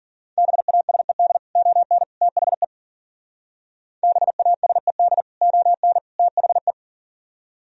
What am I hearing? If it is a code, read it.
Morse code, heard as a keyed tone, about 51 wpm, 700 Hz. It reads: BASED ON THE BASED ON THE